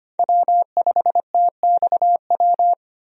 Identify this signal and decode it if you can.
Morse code, heard as a keyed tone, about 25 wpm, 690 Hz. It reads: W5TXW
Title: USA Call Signs - 5 Characters - 25wpm